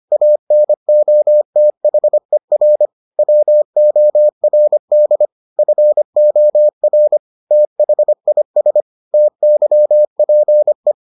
Morse code, heard as a keyed tone, about 25 wpm, 600 Hz. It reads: ANOTHER WORD FOR THIS TYPE